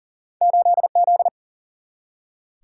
Morse code, heard as a keyed tone, about 40 wpm, 700 Hz. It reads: 8Z
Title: Single-Letter Single-Number 40wpm